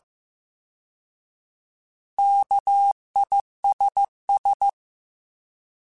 Morse code, heard as a keyed tone, about 15 wpm, 800 Hz. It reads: KISS
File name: Alphabetical / Alphabetical mp3s / Ks @13